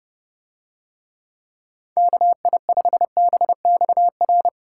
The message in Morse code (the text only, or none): KI5BXR